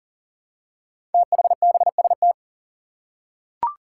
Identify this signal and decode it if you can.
Morse code, heard as a keyed tone, about 39 wpm, 700 Hz. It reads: THBST
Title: Code Groups - 5 Letters - 40wpm